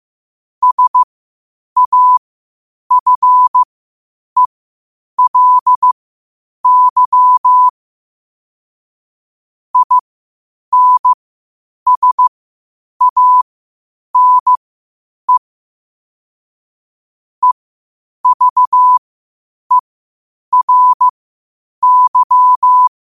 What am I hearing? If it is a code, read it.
Morse code, heard as a keyed tone, about 15 wpm, spaced as for 8 wpm, 1000 Hz. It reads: SAFELY INSANE EVERY
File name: qotdfiles / 8wpm